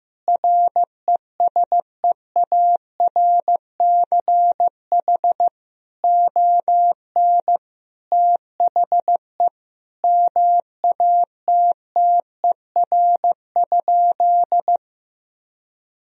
Morse code, heard as a keyed tone, about 16 wpm, 700 Hz. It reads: RESEARCH ON THE MATTER?
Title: Sentences from Top 1000 Words - Review Part 4 - 15wpm